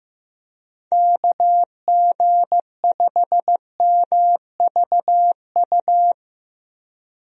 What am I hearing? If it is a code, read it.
Morse code, heard as a keyed tone, about 15 wpm, 700 Hz. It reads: KG5MVU
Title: USA Call Signs - 6 Characters - 15wpm